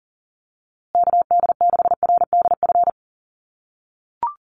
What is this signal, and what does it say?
Morse code, heard as a keyed tone, about 40 wpm, 700 Hz. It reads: KD6RDF
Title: USA Call Signs Repeated 3x 40wpm